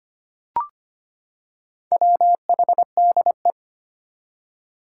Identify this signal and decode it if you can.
Morse code, heard as a keyed tone, about 25 wpm, 700 Hz. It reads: WHDE
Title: Four Challenging Letter-Number Combinations 25wpm